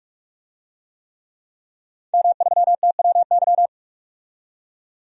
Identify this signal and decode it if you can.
Morse code, heard as a keyed tone, about 44 wpm, 700 Hz. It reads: M3TWY
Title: DXCC-100 Call Signs - 5 Characters - 45wpm